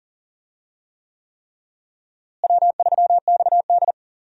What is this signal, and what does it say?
Morse code, heard as a keyed tone, about 38 wpm, 700 Hz. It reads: W3XD
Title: USA Call Signs - 4 Characters - Repeated 3x - 40wpm